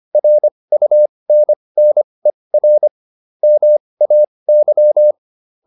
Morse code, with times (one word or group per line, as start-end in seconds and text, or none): 0.2-2.9 RUNNER
3.4-5.1 MAY